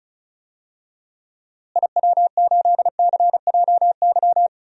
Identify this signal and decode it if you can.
Morse code, heard as a keyed tone, about 34 wpm, 700 Hz. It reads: IW8CJY
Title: DXCC-100 Call Signs - 6 Characters - Repeated 2x - 35wpm